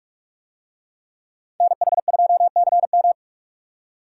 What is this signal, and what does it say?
Morse code, heard as a keyed tone, about 45 wpm, 700 Hz. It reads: NH2CM